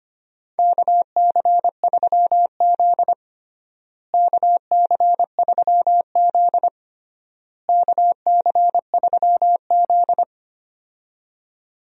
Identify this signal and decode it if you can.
Morse code, heard as a keyed tone, about 25 wpm, 700 Hz. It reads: KC3Z KC3Z KC3Z
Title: USA Call Signs - 4 Characters - Repeated 3x - 25wpm